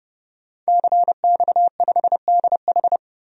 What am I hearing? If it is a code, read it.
Morse code, heard as a keyed tone, about 30 wpm, 700 Hz. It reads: CX5DH
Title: DXCC-100 Call Signs - 5 Characters - 30wpm